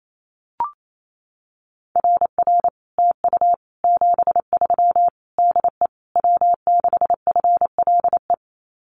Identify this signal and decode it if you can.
Morse code, heard as a keyed tone, about 28 wpm, 700 Hz. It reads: RR TU 73 DE W6FLE
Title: Common QSO Phrases Encore 28wpm